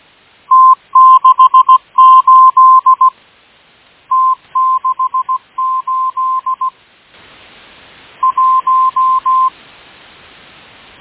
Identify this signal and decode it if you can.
Morse code, heard as a keyed tone, about 17 wpm, 1040 Hz. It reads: T68 T68 1
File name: M12- -1830utc-8157khz